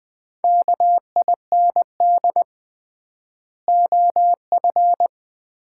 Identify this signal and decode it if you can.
Morse code, heard as a keyed tone, about 20 wpm, 690 Hz. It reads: KIND OF
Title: 3x Word-Spacing - Sets of 2 Words - 20wpm